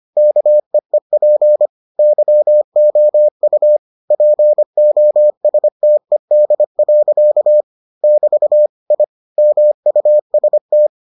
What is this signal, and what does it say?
Morse code, heard as a keyed tone, about 25 wpm, 600 Hz. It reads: KEEP YOU POSTED. <BT> I MUST